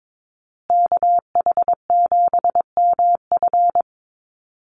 Morse code, heard as a keyed tone, about 22 wpm, 700 Hz. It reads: KH7MF
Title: USA Call Signs Repeated 2x 22wpm